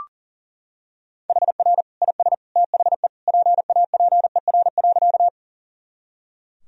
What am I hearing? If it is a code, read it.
Morse code, heard as a keyed tone, about 40 wpm, 700 Hz. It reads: HR IS THE PAPER.
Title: Sentences from Top 300 Words and Abbreviations 40wpm